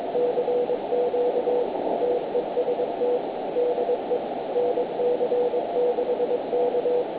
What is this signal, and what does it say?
Morse code, heard as a keyed tone, about 22 wpm, 490 Hz, with noise. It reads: CQ TEST DE NC6K
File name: nc6k 160m 0745 gmt